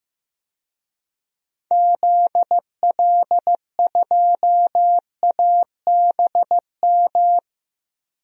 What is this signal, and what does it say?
Morse code, heard as a keyed tone, about 15 wpm, 700 Hz. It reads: ZL2ABM
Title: DXCC-100 Call Signs - 6 Characters - 15wpm